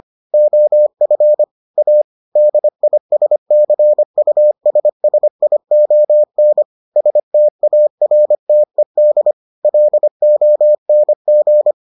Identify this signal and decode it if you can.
Morse code, heard as a keyed tone, about 25 wpm, 600 Hz. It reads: OF A DISCUSSION STARTED LONG